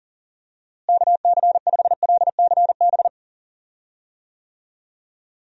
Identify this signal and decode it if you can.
Morse code, heard as a keyed tone, about 40 wpm, 700 Hz. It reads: KC5LCB
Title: USA Call Signs - 6 Characters - Repeated 2x - 40wpm